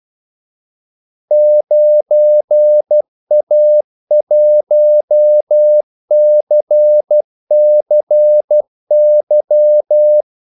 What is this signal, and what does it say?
Morse code, heard as a keyed tone, about 12 wpm, 600 Hz. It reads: 9A1CCY
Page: 2 ekstra ordmellemrum mellem kaldesignalerne